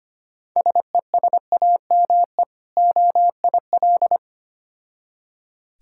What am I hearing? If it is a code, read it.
Morse code, heard as a keyed tone, about 25 wpm, 690 Hz. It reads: SESAME OIL
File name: practice / 25wpm / Difficult Words with Hints - 25wpm